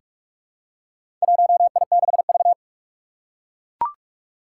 Morse code, heard as a keyed tone, about 42 wpm, 700 Hz. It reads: 1I6V
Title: Four Challenging Letter-Number Combinations 45wpm